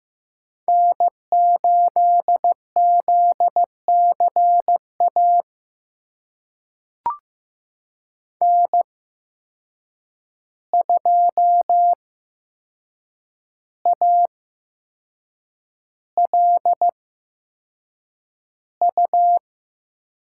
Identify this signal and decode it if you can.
Morse code, heard as a keyed tone, about 15 wpm, spaced as for 6 wpm, 700 Hz. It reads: N8ZCA N2ALU
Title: USA Call Signs - 5 Characters - 8x Character Spacing - 15wpm